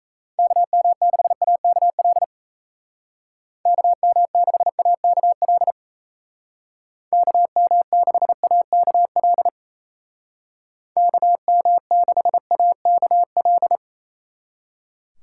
Morse code, heard as a keyed tone, about 36 wpm, 700 Hz. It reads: KM6AKL KM6AKL KM6AKL KM6AKL